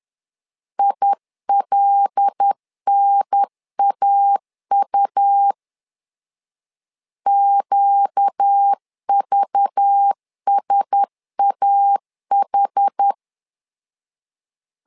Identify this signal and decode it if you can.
Morse code, heard as a keyed tone, about 10 wpm, 800 Hz. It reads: ILNAU QVSAH
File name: LEZIONI / 5 LBVC / 5 LBV 10wpm 2248